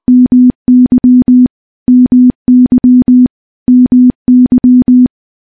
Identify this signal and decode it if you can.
Morse code, heard as a keyed tone, about 20 wpm, 250 Hz. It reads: MY MY MY
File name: morse-training-20WPM